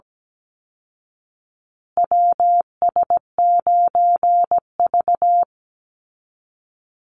Morse code, heard as a keyed tone, about 17 wpm, 700 Hz. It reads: WS9V
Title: Call Signs Repeated 2x - 17wpm